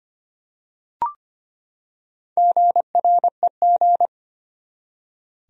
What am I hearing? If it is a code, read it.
Morse code, heard as a keyed tone, about 25 wpm, 700 Hz. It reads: GREG
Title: USA Common Names - Max 5 Letters - 25wpm